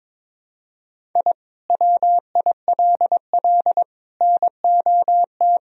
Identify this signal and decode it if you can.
Morse code, heard as a keyed tone, about 22 wpm, 700 Hz. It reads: I WILL NOT
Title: Sentences from Top 1000 Words - Review Part 1 - 22wpm